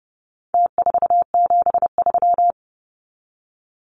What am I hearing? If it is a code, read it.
Morse code, heard as a keyed tone, about 30 wpm, 700 Hz. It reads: T473
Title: Numbers - Contest Style - 4-Digit 30wpm